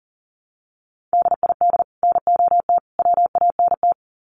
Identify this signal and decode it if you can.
Morse code, heard as a keyed tone, about 40 wpm, 700 Hz. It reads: DID NOT WANT
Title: Sets of 3 Words 40wpm